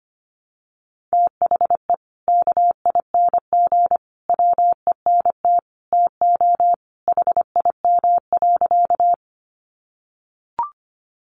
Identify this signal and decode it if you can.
Morse code, heard as a keyed tone, about 25 wpm, 700 Hz. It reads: THE KING WENT TO HIM.
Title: Sentences from Top 400 Words 25wpm